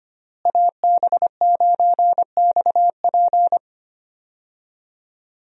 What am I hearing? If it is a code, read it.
Morse code, heard as a keyed tone, about 25 wpm, 700 Hz. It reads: AB9XP
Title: USA Call Signs - 5 Characters - Repeated 2x - 25wpm